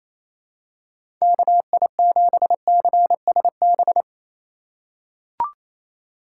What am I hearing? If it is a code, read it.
Morse code, heard as a keyed tone, about 28 wpm, 700 Hz. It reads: KI7CSB
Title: USA Call Signs - 6 Characters - 4x Character Spacing - 28wpm